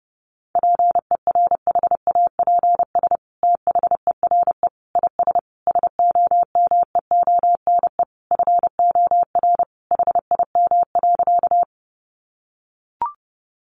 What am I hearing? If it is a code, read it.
Morse code, heard as a keyed tone, about 28 wpm, 700 Hz. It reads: PERHAPS THERE IS SOMEONE FOR HIM.
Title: Sentences from Top 600 Words 30wpm